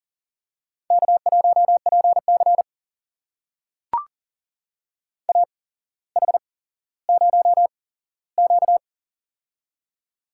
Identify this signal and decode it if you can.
Morse code, heard as a keyed tone, about 40 wpm, spaced as for 17 wpm, 700 Hz. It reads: K1PC AH0Q
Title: USA Call Signs - 8x Character Spacing - 40wpm